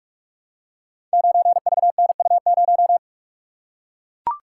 Morse code, heard as a keyed tone, about 44 wpm, 700 Hz. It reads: 9VNU0
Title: Code Groups - 5 Letter:Numbers - 45wpm